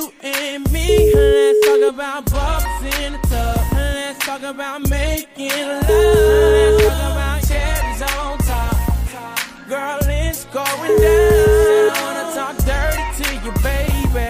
Ringback tone (Europe, 425 Hz), with noise.